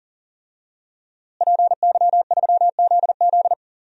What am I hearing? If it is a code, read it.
Morse code, heard as a keyed tone, about 40 wpm, 700 Hz. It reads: PY3ZZ
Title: DXCC-100 Call Signs - 5 Characters - Repeated 2x - 40wpm